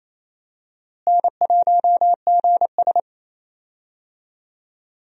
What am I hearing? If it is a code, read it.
Morse code, heard as a keyed tone, about 28 wpm, 700 Hz. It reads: N1GS
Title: USA Call Signs - 4 Characters - Repeated 2x - 28wpm